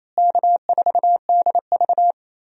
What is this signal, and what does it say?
Morse code, heard as a keyed tone, about 28 wpm, 700 Hz. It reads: K4DV
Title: USA Call Signs - 4 Characters - Repeated 2x - 28wpm